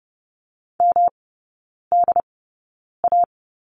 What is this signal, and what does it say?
Morse code, heard as a keyed tone, about 30 wpm, spaced as for 9 wpm, 700 Hz. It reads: MDA
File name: Lesson 18e - Letter m - Three Characters 30wpm